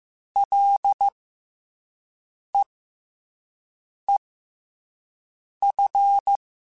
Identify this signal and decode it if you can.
Morse code, heard as a keyed tone, about 15 wpm, spaced as for 5 wpm, 800 Hz. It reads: LEEF
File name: F and L / F and L mp3 / F and L Prefixes 3x2 Twice @5